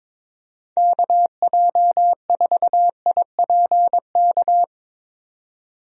Morse code, heard as a keyed tone, about 22 wpm, 700 Hz. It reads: KJ4IPK
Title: USA Call Signs - 6 Characters - 22wpm